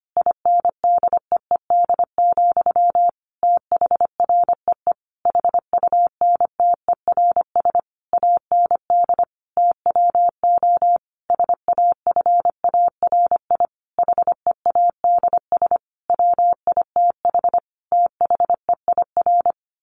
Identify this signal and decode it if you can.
Morse code, heard as a keyed tone, about 25 wpm, 700 Hz. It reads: INDEED, THREE HUNTERS AND TWO SAFARI HEADS WITH THEIR
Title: A Sound Of Thunder 25wpm